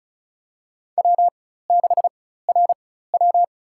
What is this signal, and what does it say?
Morse code, heard as a keyed tone, about 36 wpm, spaced as for 16 wpm, 700 Hz. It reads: W6RW
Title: USA Call Signs - 4 Characters - 4x Character Spacing - 35wpm